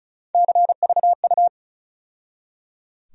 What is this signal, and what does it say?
Morse code, heard as a keyed tone, about 36 wpm, 700 Hz. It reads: CVU
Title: USA Call Sign Postfixes - 3 Letters 35wpm